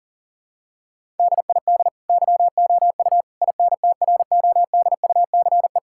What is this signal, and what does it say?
Morse code, heard as a keyed tone, about 40 wpm, 700 Hz. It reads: DID YOU INTRODUCE
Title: Sentences from Top 1600 Words 40wpm